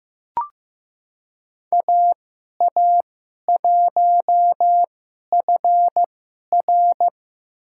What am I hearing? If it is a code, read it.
Morse code, heard as a keyed tone, about 15 wpm, spaced as for 11 wpm, 700 Hz. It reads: AA1FR
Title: USA Call Signs - 5 Characters - 2x Character Spacing - 15wpm